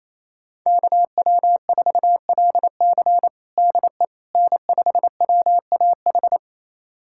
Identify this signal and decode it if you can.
Morse code, heard as a keyed tone, about 28 wpm, 700 Hz. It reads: KW4LC DE N5WAH
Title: USA Callsign DE Callsign - 5 Characters - 28wpm